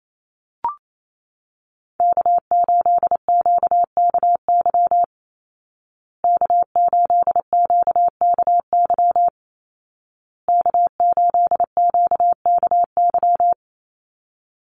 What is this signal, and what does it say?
Morse code, heard as a keyed tone, about 28 wpm, 700 Hz. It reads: K8QKY K8QKY K8QKY
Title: Call Signs Repeated 2x - 28wpm